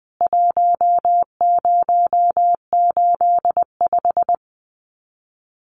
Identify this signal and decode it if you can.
Morse code, heard as a keyed tone, about 20 wpm, 700 Hz. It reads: 1085
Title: CWT Exchange 20wpm